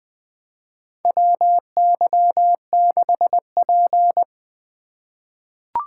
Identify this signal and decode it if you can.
Morse code, heard as a keyed tone, about 20 wpm, 700 Hz. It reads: WY6P
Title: USA Call Signs - 4 Characters - 2x Character Spacing - 20wpm